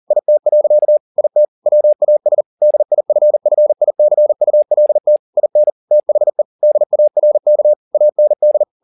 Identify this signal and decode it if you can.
Morse code, heard as a keyed tone, about 40 wpm, 600 Hz. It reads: IT. IT WAS DIFFICULT IN THE DARK AND